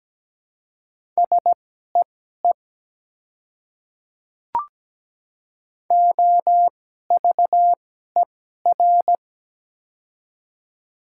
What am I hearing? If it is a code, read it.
Morse code, heard as a keyed tone, about 17 wpm, spaced as for 12 wpm, 700 Hz. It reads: SEE OVER
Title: 2x Character Spacing - Top 200 Words - 17wpm